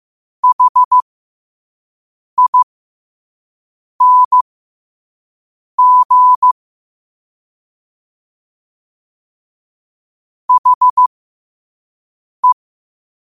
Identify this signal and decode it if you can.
Morse code, heard as a keyed tone, about 15 wpm, spaced as for 5 wpm, 1000 Hz. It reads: HING HE
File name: qotdfiles / 5wpm